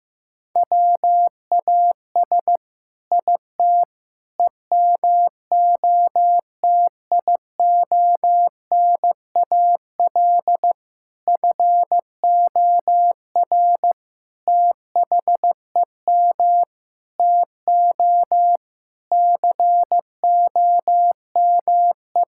Morse code, heard as a keyed tone, about 16 wpm, 700 Hz. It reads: WAS IT EMOTIONAL FOR THEM TO COME
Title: Sentences from Top 1000 Words - Review Part 5 - 15wpm